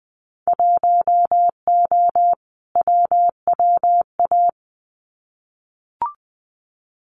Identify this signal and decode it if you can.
Morse code, heard as a keyed tone, about 20 wpm, 700 Hz. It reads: 1O WWA
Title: Winter Field Day Exchanges 20wpm